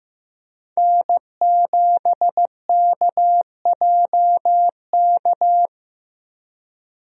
Morse code, heard as a keyed tone, about 15 wpm, 700 Hz. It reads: N7KJK
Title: USA Call Signs - 5 Characters - 15wpm